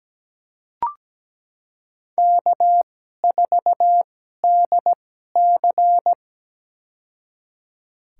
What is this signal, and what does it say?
Morse code, heard as a keyed tone, about 17 wpm, spaced as for 12 wpm, 700 Hz. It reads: K4DC
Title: USA Call Signs - 2x Character Spacing - 17wpm